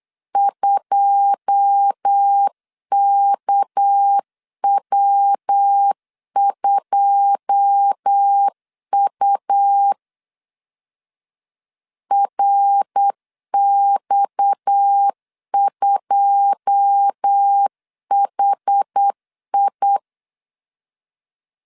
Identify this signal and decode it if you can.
Morse code, heard as a keyed tone, about 8 wpm, 800 Hz. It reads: 2KW2U RX2HI
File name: LEZIONI / 9 2 8wpm 3437